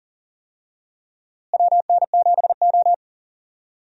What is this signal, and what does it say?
Morse code, heard as a keyed tone, about 37 wpm, 700 Hz. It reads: WN7O
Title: USA Call Signs - 4 Characters - Repeated 3x - 40wpm